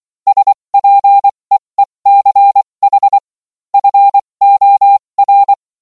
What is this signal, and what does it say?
Morse code, heard as a keyed tone, about 24 wpm, spaced as for 20 wpm, 800 Hz. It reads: SPEECH FOR